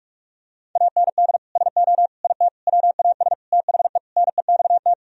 Morse code, heard as a keyed tone, about 46 wpm, 700 Hz. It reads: AND SO IT WAS THE NEXT
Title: Sentences from Top 1000 Words - Review Part 1 - 45wpm